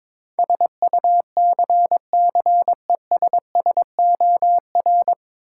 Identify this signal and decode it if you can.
Morse code, heard as a keyed tone, about 22 wpm, 700 Hz. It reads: SUCCESSOR
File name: 9 Letter Words - First 3 Letters Given - 22wpm